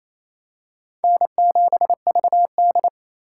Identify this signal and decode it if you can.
Morse code, heard as a keyed tone, about 28 wpm, 700 Hz. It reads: N7VD
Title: USA Call Signs - 4 Characters - 3x Character Spacing - 28wpm